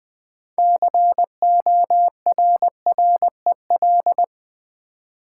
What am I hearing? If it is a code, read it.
Morse code, heard as a keyed tone, about 20 wpm, 700 Hz. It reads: CORREL
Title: Build A Word - 3 Letters At A Time - 9-Letter Words - 20wpm